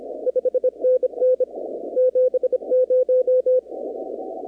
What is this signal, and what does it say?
Morse code, heard as a keyed tone, about 27 wpm, 510 Hz. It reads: 5NN 70
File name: contests / 2019-cw-wpx-kt5j-k5pi- -2330 0-23